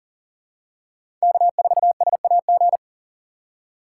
Morse code, heard as a keyed tone, about 40 wpm, 700 Hz. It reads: K4SAG
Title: USA Call Signs - 5 Characters - 3x Character Spacing - 40wpm